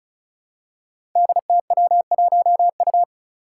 Morse code, heard as a keyed tone, about 36 wpm, 700 Hz. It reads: DTW1U
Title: Five Challenging Letter-Number Combinations 35wpm